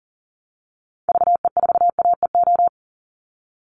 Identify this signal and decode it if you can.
Morse code, heard as a keyed tone, about 40 wpm, 700 Hz. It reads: VE4AEO
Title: Call Signs Repeated 3x - 40wpm